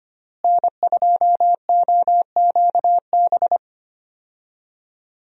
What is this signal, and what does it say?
Morse code, heard as a keyed tone, about 25 wpm, 700 Hz. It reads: N2OQB
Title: USA Call Signs - 5 Characters - Repeated 3x - 25wpm